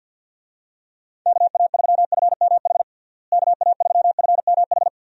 Morse code, heard as a keyed tone, about 49 wpm, 700 Hz. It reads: KA3FMH KA3FMH
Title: USA Call Signs - 6 Characters - Repeated 2x - 50wpm